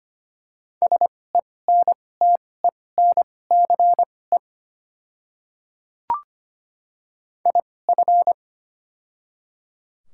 Morse code, heard as a keyed tone, about 25 wpm, spaced as for 18 wpm, 700 Hz. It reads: SENTENCE IF
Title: 2x Character Spacing - Top 200 Words - 25wpm